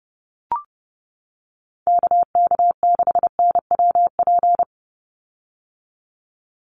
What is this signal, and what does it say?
Morse code, heard as a keyed tone, about 30 wpm, 700 Hz. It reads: KK6NWP
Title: USA Call Signs 30wpm